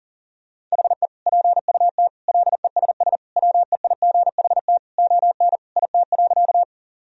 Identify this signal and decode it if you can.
Morse code, heard as a keyed tone, about 40 wpm, 700 Hz. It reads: HE PUT LESS WEIGHT ON IT.
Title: Sentences from Top 1000 Words - Review Part 3 - 40wpm